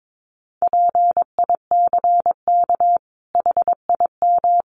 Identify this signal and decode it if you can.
Morse code, heard as a keyed tone, about 22 wpm, 700 Hz. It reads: PICK HIM
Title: Sentences from Top 1000 Words 22wpm